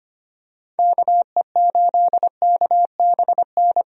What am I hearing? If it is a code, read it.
Morse code, heard as a keyed tone, about 25 wpm, 700 Hz. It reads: KE8KBN
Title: USA Call Signs - 6 Characters - 25wpm